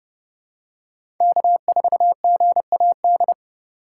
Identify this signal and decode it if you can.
Morse code, heard as a keyed tone, about 32 wpm, 700 Hz. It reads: K4GAD
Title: USA Call Signs - 5 Characters - 4x Character Spacing - 30wpm